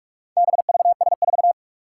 Morse code, heard as a keyed tone, about 41 wpm, 700 Hz. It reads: BVS4
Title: Four Challenging Letter-Number Combinations 45wpm